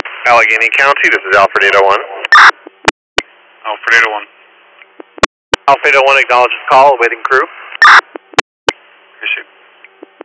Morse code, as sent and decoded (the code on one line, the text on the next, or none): none